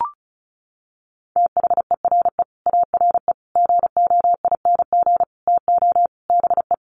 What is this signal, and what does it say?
Morse code, heard as a keyed tone, about 34 wpm, 700 Hz. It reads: THERE ARE GOING TO BE
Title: Sets of 5 Words 35wpm